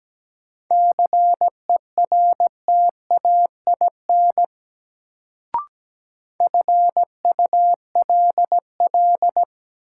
Morse code, heard as a keyed tone, about 17 wpm, 700 Hz. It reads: CERTAIN FULL
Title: Sentences from Top 500 Words - New Word Drill - 17wpm